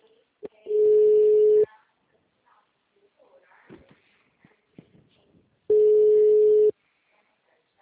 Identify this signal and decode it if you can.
Ringback tone (Europe, 425 Hz), with speech.